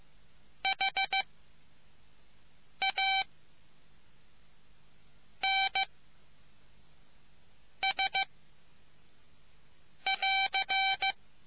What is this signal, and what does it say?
Morse code, heard as a keyed tone, about 15 wpm, 780 Hz. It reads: H A N S <AR>